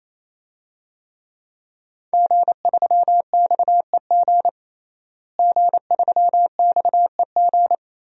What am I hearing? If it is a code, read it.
Morse code, heard as a keyed tone, about 28 wpm, 700 Hz. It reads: G3XEG G3XEG
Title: DXCC-100 Call Signs - 5 Characters - Repeated 3x - 28wpm